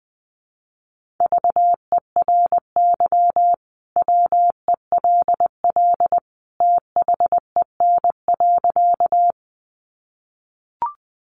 Morse code, heard as a keyed tone, about 20 wpm, 700 Hz. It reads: VERY WELL THEN.
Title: Sentences from Top 200 Words 20wpm